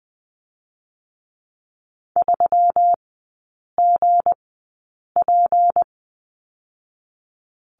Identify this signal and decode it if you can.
Morse code, heard as a keyed tone, about 20 wpm, 700 Hz. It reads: 3 G P
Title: 172 Lesson 28e - Number 9 - Three Characters 20wpm